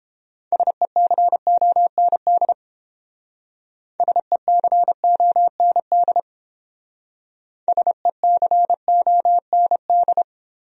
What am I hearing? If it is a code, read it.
Morse code, heard as a keyed tone, about 31 wpm, 700 Hz. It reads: SECOND SECOND SECOND